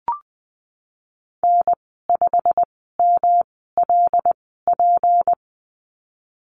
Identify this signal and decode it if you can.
Morse code, heard as a keyed tone, about 19 wpm, spaced as for 14 wpm, 700 Hz. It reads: N5MLP
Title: 173 Lesson 28g - Number 9 - Call Signs 20wpm